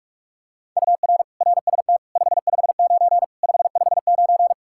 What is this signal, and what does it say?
Morse code, heard as a keyed tone, about 46 wpm, 700 Hz. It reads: UR RST 559 559
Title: Practice QSO - Exchange 1 - Beginning - 45wpm